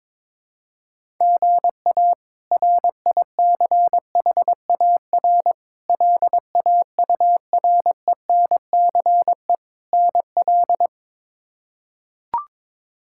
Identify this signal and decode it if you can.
Morse code, heard as a keyed tone, about 22 wpm, 700 Hz. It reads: GA RICHAR LAURENCE NL
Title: K1USN SST QSO Practice - 22wpm